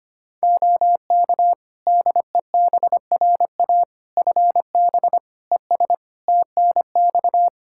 Morse code, heard as a keyed tone, about 25 wpm, 700 Hz. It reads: OK DEBRA FB ES TNX
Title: Practice QSO - Exchange 2 - Info - 25wpm